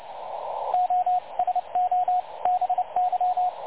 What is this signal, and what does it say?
Morse code, heard as a keyed tone, about 29 wpm, spaced as for 25 wpm, 730 Hz, with radio noise. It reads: OSOBY
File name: dlazaawansowanych / nagraniazpasma / websdr recording 3565.0kHz